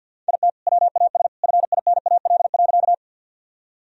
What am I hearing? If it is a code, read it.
Morse code, heard as a keyed tone, about 46 wpm, 700 Hz. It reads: IT WAS FINAL.
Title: Sentences from Top 1000 Words - Review Part 2 - 50wpm